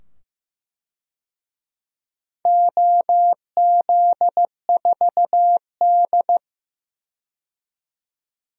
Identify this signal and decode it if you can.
Morse code, heard as a keyed tone, about 15 wpm, 700 Hz. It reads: OZ4D
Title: DXCC-100 Call Signs - 4 Characters - 15wpm